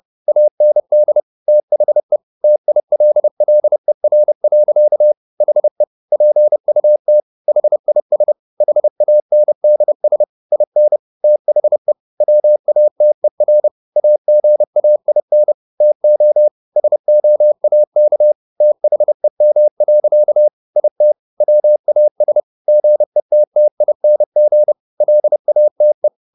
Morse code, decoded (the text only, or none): AND THE TILLER. HE PUT HIS HANDS IN THE WATER AGAIN TO SOAK THEM. IT WAS GETTING LATE